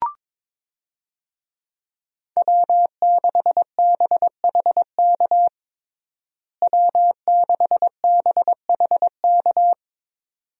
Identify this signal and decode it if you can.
Morse code, heard as a keyed tone, about 21 wpm, 700 Hz. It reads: W6BHK W6BHK
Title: USA Call Signs - 5 Characters - Repeated 2x - 22wpm